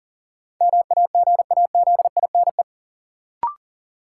Morse code, heard as a keyed tone, about 40 wpm, 700 Hz. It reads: MAGAZINE
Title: Sentences from Top 1000 Words - Review Part 7 - New Word Drill - 40wpm